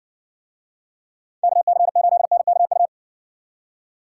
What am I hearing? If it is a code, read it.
Morse code, heard as a keyed tone, about 59 wpm, 700 Hz. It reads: KC8NKU